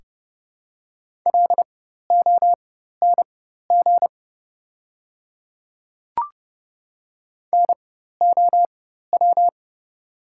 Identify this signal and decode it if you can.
Morse code, heard as a keyed tone, about 30 wpm, spaced as for 14 wpm, 700 Hz. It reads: LONG NOW
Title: 4x Character Spacing - Top 100 Words - 30wpm